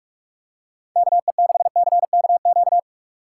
Morse code, heard as a keyed tone, about 48 wpm, 700 Hz. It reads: KE6CKQ
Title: USA Call Signs - 6 Characters - 4x Character Spacing - 45wpm